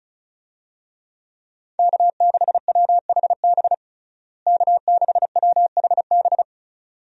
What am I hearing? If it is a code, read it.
Morse code, heard as a keyed tone, about 34 wpm, 700 Hz. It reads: K6WHB K6WHB
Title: USA Call Signs - 5 Characters - Repeated 2x - 35wpm